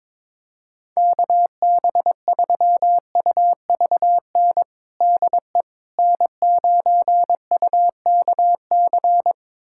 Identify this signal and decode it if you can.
Morse code, heard as a keyed tone, about 21 wpm, 700 Hz. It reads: KB3UVN DE N9UKC
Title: USA Callsign DE Callsign 22wpm